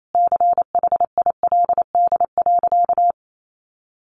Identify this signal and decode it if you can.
Morse code, heard as a keyed tone, about 28 wpm, 700 Hz. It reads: CHILD.
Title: Sentences from Top 1300 Words 28wpm